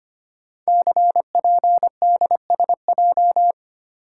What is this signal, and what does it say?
Morse code, heard as a keyed tone, about 25 wpm, 700 Hz. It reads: CPDSJ
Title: Code Groups - 5 Letter:Numbers - 25wpm